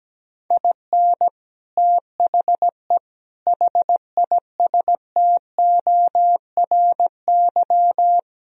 Morse code, heard as a keyed tone, about 17 wpm, 700 Hz. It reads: IN THE HISTORY
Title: Speed-Racing - Sets of 4 Words 17wpm